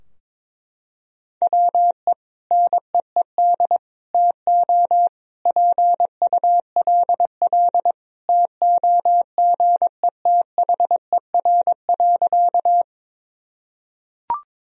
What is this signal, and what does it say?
Morse code, heard as a keyed tone, about 22 wpm, 700 Hz. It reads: WE NEED TO PULL TOGETHER.
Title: Sentences from Top 1000 Words - Review Part 2 - 22wpm